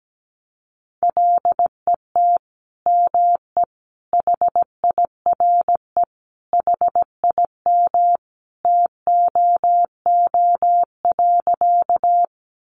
Morse code, heard as a keyed tone, about 17 wpm, 700 Hz. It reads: LET ME HIRE HIM TOO.
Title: Sentences from Top 1400 Words 17wpm